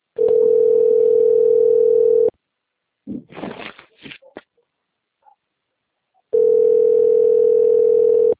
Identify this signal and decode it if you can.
Ringback tone (North America), 440 + 480 Hz.